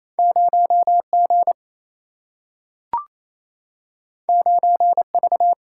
Morse code, heard as a keyed tone, about 28 wpm, 700 Hz. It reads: 0G 9V
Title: Single-Letter Single-Number 28wpm